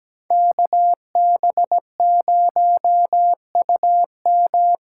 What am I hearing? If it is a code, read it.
Morse code, heard as a keyed tone, about 17 wpm, 700 Hz. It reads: KB0UM
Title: USA Call Signs - 6 Characters - 8x Character Spacing - 17wpm